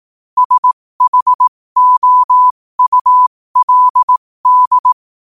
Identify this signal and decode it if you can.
Morse code, heard as a keyed tone, about 18 wpm, spaced as for 15 wpm, 1000 Hz. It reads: SHOULD
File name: qotdfiles / 15wpm